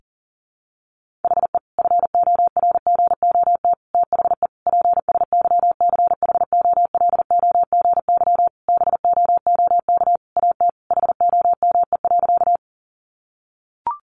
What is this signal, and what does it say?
Morse code, heard as a keyed tone, about 40 wpm, 700 Hz. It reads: HE FORGOT THE PSYCHOLOGY BOOK AT HOME.
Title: Sentences from Top 1400 Words 40wpm